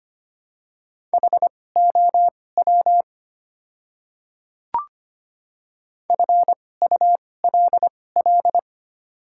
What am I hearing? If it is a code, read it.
Morse code, heard as a keyed tone, about 25 wpm, spaced as for 18 wpm, 700 Hz. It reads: HOW FULL
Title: 2x Character Spacing - Top 500 Words - 25wpm